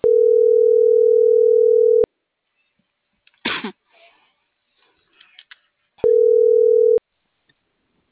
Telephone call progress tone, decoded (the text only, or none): ringback tone